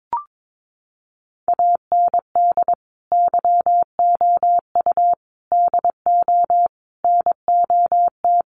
Morse code, heard as a keyed tone, about 22 wpm, 700 Hz. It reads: AND YOU DO NOT
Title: Sets of 5 Words Encore2 22wpm